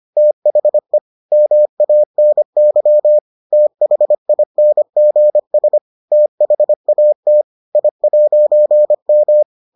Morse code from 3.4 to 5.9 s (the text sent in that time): THINGS